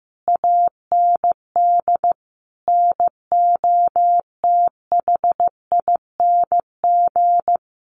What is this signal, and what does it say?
Morse code, heard as a keyed tone, about 15 wpm, 700 Hz. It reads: AND NOTHING
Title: Sentences from Top 800 Words 15wpm